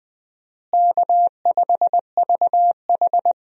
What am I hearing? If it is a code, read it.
Morse code, heard as a keyed tone, about 20 wpm, 700 Hz. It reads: K5VH
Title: USA Call Signs - 4 Characters - Repeated 3x - 20wpm